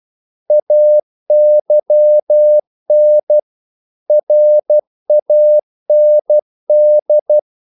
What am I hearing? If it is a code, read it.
Morse code, heard as a keyed tone, about 12 wpm, 600 Hz. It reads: AYN RAND
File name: CW / Practice / EnglishQuotations0112